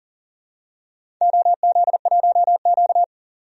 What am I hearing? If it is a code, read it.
Morse code, heard as a keyed tone, about 40 wpm, 700 Hz. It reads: OZ1Q
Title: DXCC-100 Call Signs - 4 Characters - 40wpm